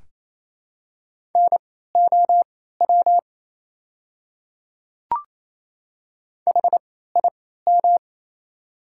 Morse code, heard as a keyed tone, about 28 wpm, spaced as for 16 wpm, 700 Hz. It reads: NOW HIM
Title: 3x Character Spacing - Top 100 Words - 28wpm